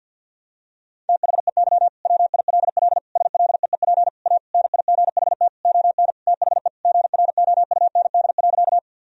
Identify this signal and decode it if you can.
Morse code, heard as a keyed tone, about 50 wpm, 700 Hz. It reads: THEY WILL SLEEP A NIGHT ON THE GROUND.
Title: Sentences from Top 1000 Words - Review Part 2 - 50wpm